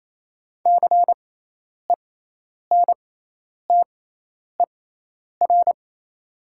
Morse code, heard as a keyed tone, about 27 wpm, spaced as for 10 wpm, 700 Hz. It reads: CENTER
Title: 6x Character Spacing - Top 500 Words - 28wpm